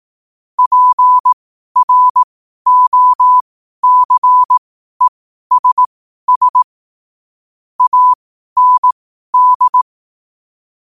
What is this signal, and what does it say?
Morse code, heard as a keyed tone, about 18 wpm, spaced as for 12 wpm, 1000 Hz. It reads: PROCESS AND
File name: 2025-11-08-12wpm